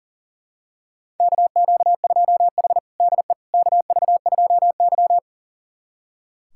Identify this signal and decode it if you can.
Morse code, heard as a keyed tone, about 40 wpm, 700 Hz. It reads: KQ2H DE KV2Y
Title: USA Callsign DE Callsign - 4 Characters - 40wpm